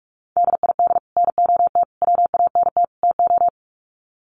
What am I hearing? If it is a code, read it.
Morse code, heard as a keyed tone, about 44 wpm, 700 Hz. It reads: DID NOT WANT TO
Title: Sets of 4 Words 45wpm